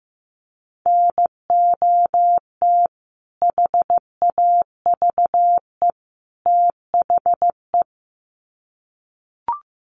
Morse code, heard as a keyed tone, about 15 wpm, 700 Hz. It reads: NOT HAVE THE
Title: Sets of 3 Words 15wpm